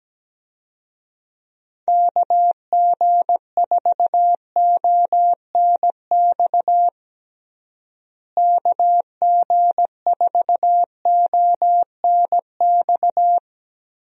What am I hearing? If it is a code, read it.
Morse code, heard as a keyed tone, about 17 wpm, 700 Hz. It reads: KG4ONX KG4ONX
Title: USA Call Signs - 6 Characters - Repeated 2x - 17wpm